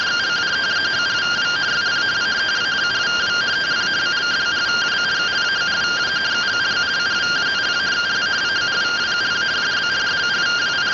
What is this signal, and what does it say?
Two-tone teleprinter signal, about 170 Hz shift.